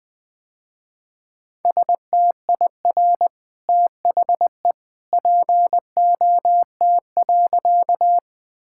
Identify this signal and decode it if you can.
Morse code, heard as a keyed tone, about 20 wpm, 700 Hz. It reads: STIR THE POT.
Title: Easy Sentences with Hints - 20wpm